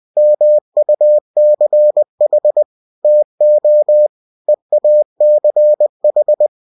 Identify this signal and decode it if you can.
Morse code, heard as a keyed tone, about 20 wpm, 600 Hz. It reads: MUCH TO EACH
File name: cw-books / winnie-the-pooh / winnie-20wpm-page-0034